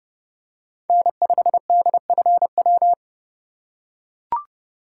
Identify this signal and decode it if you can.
Morse code, heard as a keyed tone, about 30 wpm, 700 Hz. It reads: N5DFW
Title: USA Call Signs - 5 Characters - 4x Character Spacing - 30wpm